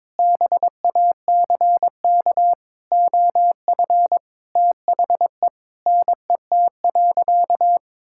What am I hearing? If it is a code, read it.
Morse code, heard as a keyed tone, about 21 wpm, 700 Hz. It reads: BACK OF THE NET.
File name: practice / 22wpm / Sentences with 8-Word Phrases - 22wpm